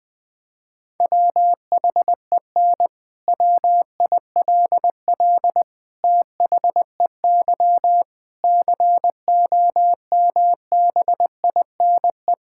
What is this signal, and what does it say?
Morse code, heard as a keyed tone, about 21 wpm, 700 Hz. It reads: WHEN WILL THEY COMBINE
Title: Sentences from Top 1600 Words 20wpm